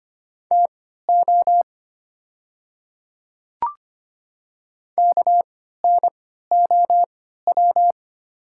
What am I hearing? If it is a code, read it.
Morse code, heard as a keyed tone, about 25 wpm, spaced as for 14 wpm, 700 Hz. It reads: TO KNOW
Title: 3x Character Spacing - Top 200 Words - 25wpm